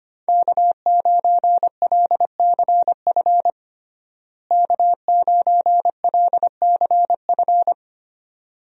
Morse code, heard as a keyed tone, about 25 wpm, 700 Hz. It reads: K9LCF K9LCF
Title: USA Call Signs - 5 Characters - Repeated 2x - 25wpm